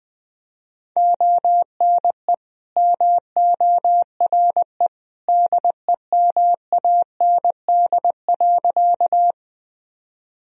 Morse code, heard as a keyed tone, about 20 wpm, 700 Hz. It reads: ONE MORE DEMAND.
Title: Sentences from Top 1000 Words - Review Part 4 - 20wpm